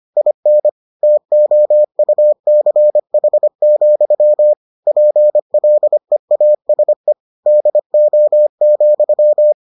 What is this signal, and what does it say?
Morse code, heard as a keyed tone, about 25 wpm, 600 Hz. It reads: IN TOUCH, PLEASE DO,